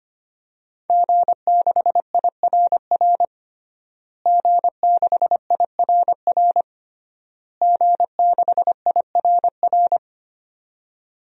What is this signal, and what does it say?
Morse code, heard as a keyed tone, about 25 wpm, 700 Hz. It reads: G6IRR G6IRR G6IRR
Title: DXCC-100 Call Signs - 5 Characters - Repeated 3x - 25wpm